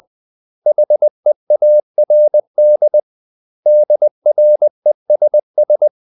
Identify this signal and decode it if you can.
Morse code, heard as a keyed tone, about 20 wpm, 600 Hz. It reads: HEARD DRESS
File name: CW / Practice / 500MostCommonEnglishWords20